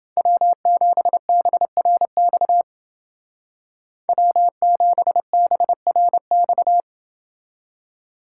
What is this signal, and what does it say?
Morse code, heard as a keyed tone, about 29 wpm, 700 Hz. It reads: W7BRX W7BRX